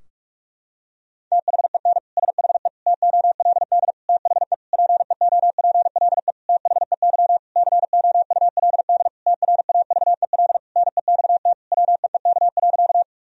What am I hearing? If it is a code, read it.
Morse code, heard as a keyed tone, about 46 wpm, 700 Hz. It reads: THEN SHE TOLD THE PEOPLE THEY COULD TRAVEL NEXT WEEK.
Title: Sentences from Top 1000 Words - Review Part 2 - 45wpm